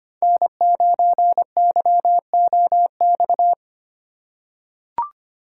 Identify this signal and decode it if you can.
Morse code, heard as a keyed tone, about 25 wpm, 700 Hz. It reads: N9YOX
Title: USA Call Signs - 5 Characters - 3x Character Spacing - 25wpm